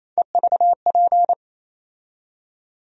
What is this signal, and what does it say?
Morse code, heard as a keyed tone, about 28 wpm, 700 Hz. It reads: EVP
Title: USA Call Sign Postfixes - 3 Letters 28wpm